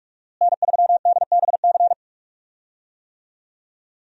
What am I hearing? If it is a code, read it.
Morse code, heard as a keyed tone, about 46 wpm, 700 Hz. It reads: N3DBC
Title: USA Call Signs - 5 Characters - Repeated 2x - 45wpm